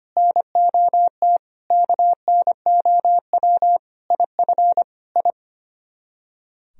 Morse code, heard as a keyed tone, about 32 wpm, 700 Hz. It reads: NOT KNOW IF I